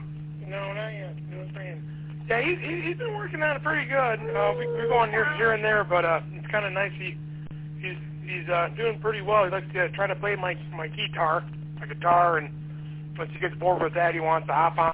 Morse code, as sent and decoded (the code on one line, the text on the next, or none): none